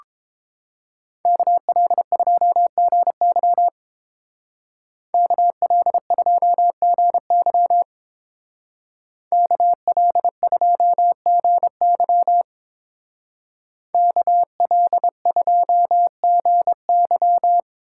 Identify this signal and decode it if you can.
Morse code, heard as a keyed tone, about 27 wpm, 700 Hz. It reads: KL2GY KL2GY KL2GY KL2GY
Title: Speed Racing - USA 2x2 Call Signs 22wpm